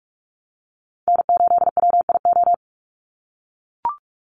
Morse code, heard as a keyed tone, about 44 wpm, 700 Hz. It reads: N8WIO
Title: USA Call Signs 45wpm